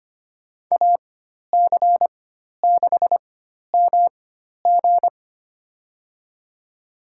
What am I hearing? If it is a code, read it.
Morse code, heard as a keyed tone, about 25 wpm, spaced as for 12 wpm, 700 Hz. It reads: AC6MG
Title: USA Call Signs - 4x Character Spacing - 25wpm